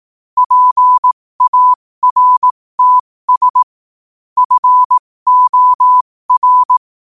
Morse code, heard as a keyed tone, about 18 wpm, spaced as for 15 wpm, 1000 Hz. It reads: PARTS FOR
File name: qotdfiles / 15wpm